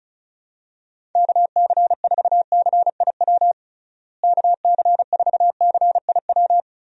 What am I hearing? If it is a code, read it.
Morse code, heard as a keyed tone, about 34 wpm, 700 Hz. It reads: KC4CIW KC4CIW
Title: USA Call Signs - 6 Characters - Repeated 2x - 35wpm